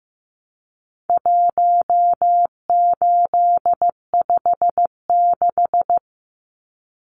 Morse code, heard as a keyed tone, about 15 wpm, 700 Hz. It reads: 1856
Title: Numbers - 4-Digit 15wpm